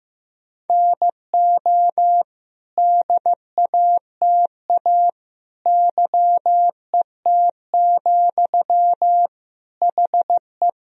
Morse code, heard as a keyed tone, about 15 wpm, 700 Hz. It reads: NO DATA YET, HE
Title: Sentences from Top 1000 Words - Review Part 3 - 15wpm